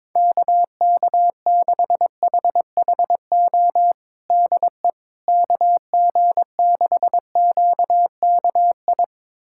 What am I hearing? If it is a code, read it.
Morse code, heard as a keyed tone, about 22 wpm, 700 Hz. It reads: KK6HHO DE KG6QKI
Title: USA Callsign DE Callsign 22wpm